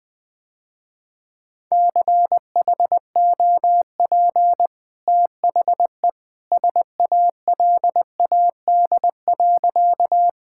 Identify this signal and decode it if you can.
Morse code, heard as a keyed tone, about 20 wpm, 700 Hz. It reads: CHOP THE SALAD.
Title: Easy Sentences with Hints - 20wpm